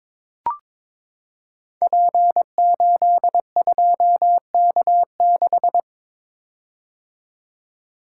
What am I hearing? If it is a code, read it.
Morse code, heard as a keyed tone, about 22 wpm, 700 Hz. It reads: P82K6
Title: Code Groups - 5 Letter:Numbers - 22wpm